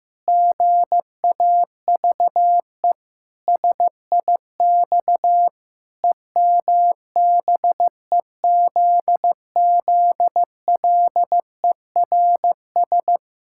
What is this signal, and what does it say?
Morse code, heard as a keyed tone, about 15 wpm, 700 Hz. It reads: GAVE SIX EMBEZZLERS
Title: Pangram Challenge v2 - 15wpm